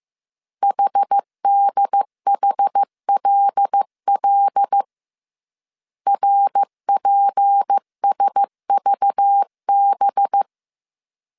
Morse code, heard as a keyed tone, about 15 wpm, 800 Hz. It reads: HDHLL RPSVB